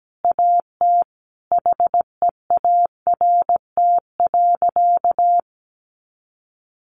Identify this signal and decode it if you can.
Morse code, heard as a keyed tone, about 17 wpm, 700 Hz. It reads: AT HEART.
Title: Sentences from Top 1500 Words 17wpm